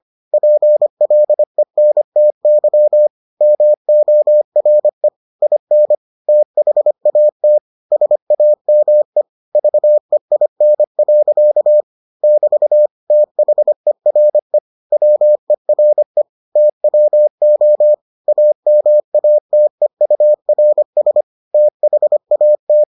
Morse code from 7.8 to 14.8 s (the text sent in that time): SAME VEIN. <BT> THERE